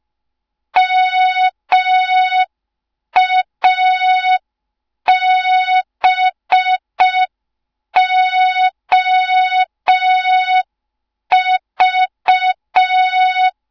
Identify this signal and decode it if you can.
Morse code, heard as a keyed tone, about 5 wpm, 3700 Hz. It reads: MABOV
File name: Morze-HA2MN / hang / 07 GYAKORLAT